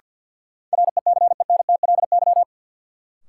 Morse code, heard as a keyed tone, about 49 wpm, 700 Hz. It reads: RECENTLY
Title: Sentences from Top 1200 Words - New Word Drill - 50wpm